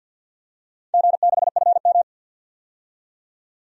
Morse code, heard as a keyed tone, about 52 wpm, 700 Hz. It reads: G6FM
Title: DXCC-100 Call Signs - 4 Characters - 50wpm